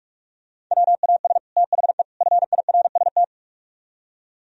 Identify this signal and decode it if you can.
Morse code, heard as a keyed tone, about 44 wpm, 700 Hz. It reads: WAS THE FIRST